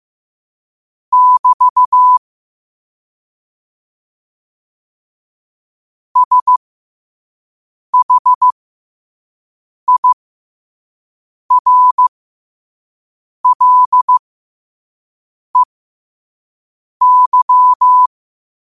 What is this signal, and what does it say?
Morse code, heard as a keyed tone, about 15 wpm, spaced as for 5 wpm, 1000 Hz. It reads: <BT> SHIRLEY
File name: qotdfiles / 5wpm